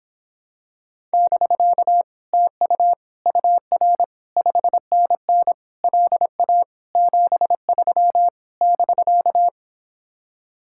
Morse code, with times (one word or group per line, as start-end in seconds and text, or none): 1.1-2.0 <BK>
2.3-2.9 TU
3.3-4.0 UR
4.4-5.5 5NN
5.8-6.6 LA
7.0-8.3 73
8.6-9.5 <BK>